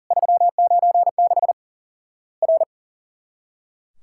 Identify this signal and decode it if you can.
Morse code, heard as a keyed tone, about 40 wpm, 700 Hz. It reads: 396 R
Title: Worked All Europe DX Contest - QTC Exchange - 40wpm